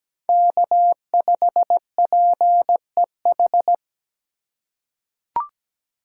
Morse code, heard as a keyed tone, about 17 wpm, 700 Hz. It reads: K5PEH
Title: USA Call Signs - 4x Character Spacing - 17wpm